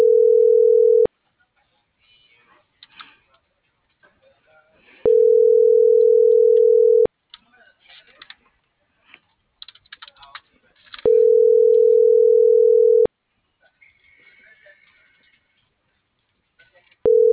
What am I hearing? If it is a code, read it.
Ringback tone (North America), 440 + 480 Hz.